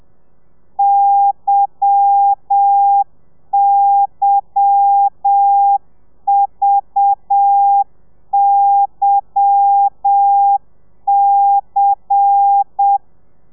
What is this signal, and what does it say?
Morse code, heard as a keyed tone, about 7 wpm, 800 Hz. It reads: YYVYC
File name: telegrafia / i2tfj / Lez10 30